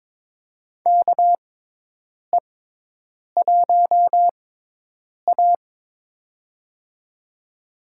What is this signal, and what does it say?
Morse code, heard as a keyed tone, about 21 wpm, spaced as for 8 wpm, 700 Hz. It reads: KE1A
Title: USA Call Signs - 4 Characters - 6x Character Spacing - 22wpm